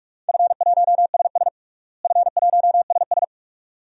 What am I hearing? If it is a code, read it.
Morse code, heard as a keyed tone, about 42 wpm, 700 Hz. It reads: F1SS F1SS
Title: DXCC-100 Call Signs - 4 Characters - Repeated 2x - 45wpm